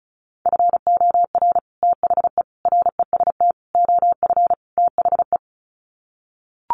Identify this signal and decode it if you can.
Morse code, heard as a keyed tone, about 34 wpm, 700 Hz. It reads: FOR THE REST OF THE
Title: Sets of 5 Words 35wpm